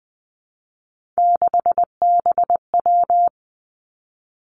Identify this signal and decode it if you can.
Morse code, heard as a keyed tone, about 19 wpm, 700 Hz. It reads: 6BW
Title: Three Challenging Letter-Number Combinations 20wpm